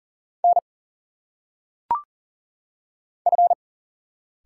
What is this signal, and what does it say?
Morse code, heard as a keyed tone, about 38 wpm, 700 Hz. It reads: N F
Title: Single Letters - Phonetic Pronunciation 40wpm